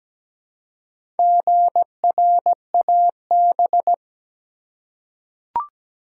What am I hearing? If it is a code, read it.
Morse code, heard as a keyed tone, about 17 wpm, 700 Hz. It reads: GRAB
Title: Four Letter Words - 17wpm